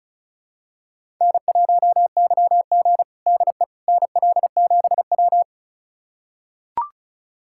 Morse code, heard as a keyed tone, about 36 wpm, 700 Hz. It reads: N1YG DE NL7W
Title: USA Callsign DE Callsign - 4 Characters - 35wpm